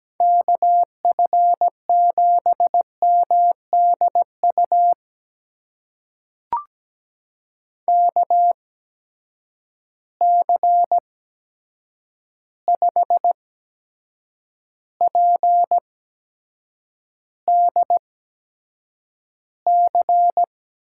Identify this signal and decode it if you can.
Morse code, heard as a keyed tone, about 17 wpm, spaced as for 7 wpm, 700 Hz. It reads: KF7MDU KC5PDC
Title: USA Call Signs - 6 Characters - 8x Character Spacing - 17wpm